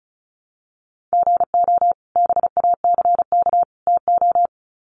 Morse code, heard as a keyed tone, about 34 wpm, 700 Hz. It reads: GO BACK TO
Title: Sets of 3 Words 35wpm